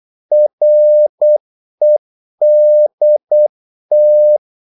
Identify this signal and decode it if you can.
Morse code, heard as a keyed tone, about 8 wpm, 600 Hz. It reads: REDT
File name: Korte ord DK 8wpm